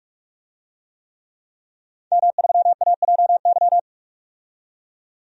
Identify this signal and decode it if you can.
Morse code, heard as a keyed tone, about 46 wpm, 700 Hz. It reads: M3AJY
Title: DXCC-100 Call Signs - 5 Characters - 45wpm